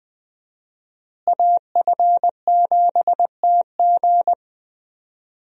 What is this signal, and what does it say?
Morse code, heard as a keyed tone, about 21 wpm, 700 Hz. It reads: AF7TG
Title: USA Call Signs - 5 Characters - Repeated 3x - 20wpm